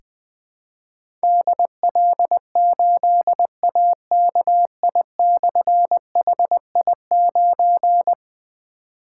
Morse code, heard as a keyed tone, about 20 wpm, 700 Hz. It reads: DL8AKI/HI9
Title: Difficult Call Signs 20wpm